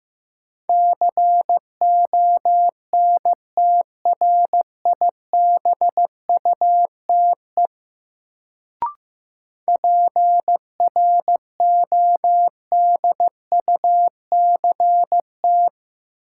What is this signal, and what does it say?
Morse code, heard as a keyed tone, about 15 wpm, 700 Hz. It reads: CONTRIBUTE PRODUCT
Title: Sentences from Top 1400 Words - New Word Drill Alt - 15wpm